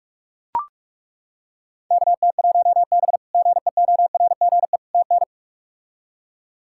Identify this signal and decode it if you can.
Morse code, heard as a keyed tone, about 41 wpm, 700 Hz. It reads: KT1B GEORGE TN
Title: K1USN SST QSO Practice - 45wpm